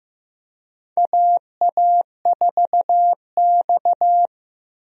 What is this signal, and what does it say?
Morse code, heard as a keyed tone, about 15 wpm, 700 Hz. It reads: AA4X
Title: USA Call Signs - 4 Characters - Repeated 3x - 15wpm